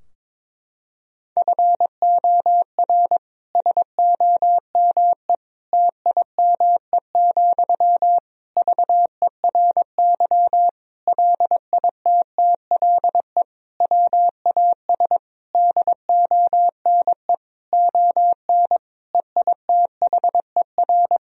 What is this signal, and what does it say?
Morse code, heard as a keyed tone, about 22 wpm, 700 Hz. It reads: FOR SOME TIME, VERY LITTLE WAS DONE ON EITHER
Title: Sentences from Top 1000 Words - Review Part 4 - 22wpm